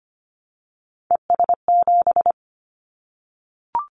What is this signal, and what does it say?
Morse code, heard as a keyed tone, about 25 wpm, 700 Hz. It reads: ES7
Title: Callsign Prefixes for Top 100 DXCC - 25wpm